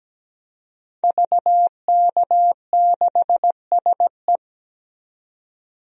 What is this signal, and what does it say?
Morse code, heard as a keyed tone, about 17 wpm, 700 Hz. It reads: VK6SE
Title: DXCC-100 Call Signs - 5 Characters - 17wpm